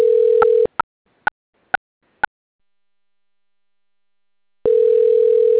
Ringback tone (North America), 440 + 480 Hz.